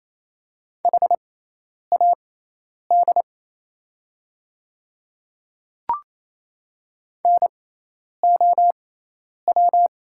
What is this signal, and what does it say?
Morse code, heard as a keyed tone, about 28 wpm, spaced as for 10 wpm, 690 Hz. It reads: HAD NOW
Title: 6x Character Spacing - Top 100 Words - 28wpm